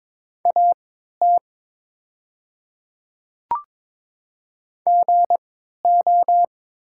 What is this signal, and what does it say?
Morse code, heard as a keyed tone, about 22 wpm, spaced as for 13 wpm, 700 Hz. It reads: AT GO
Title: 3x Character Spacing - Top 100 Words - 22wpm